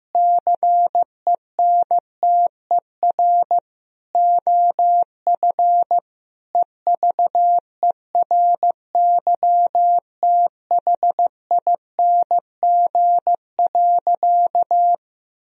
Morse code, heard as a keyed tone, about 15 wpm, 700 Hz. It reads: CENTER OF EVERYTHING.
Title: Sentences from Top 1000 Words - Review Part 2 - 15wpm